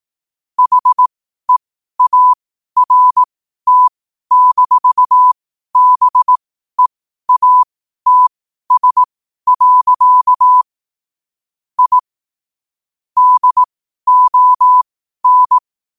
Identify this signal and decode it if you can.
Morse code, heard as a keyed tone, about 18 wpm, spaced as for 12 wpm, 1000 Hz. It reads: HEART-BEATS. I DON
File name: qotdfiles / 12wpm